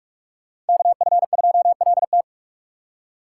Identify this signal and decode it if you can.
Morse code, heard as a keyed tone, about 58 wpm, 700 Hz. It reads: KF2LT